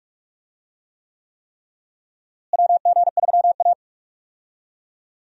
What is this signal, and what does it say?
Morse code, heard as a keyed tone, about 44 wpm, 700 Hz. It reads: WG3A
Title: USA Call Signs - 4 Characters - Repeated 3x - 45wpm